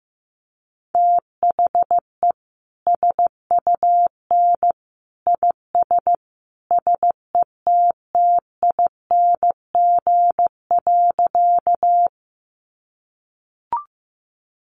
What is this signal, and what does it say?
Morse code, heard as a keyed tone, about 15 wpm, 700 Hz. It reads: THE SUN IS SETTING.
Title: Sentences from Top 1300 Words 15wpm